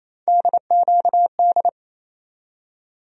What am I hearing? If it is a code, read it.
Morse code, heard as a keyed tone, about 28 wpm, 700 Hz. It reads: DQD
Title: USA Call Sign Postfixes - 3 Letters 28wpm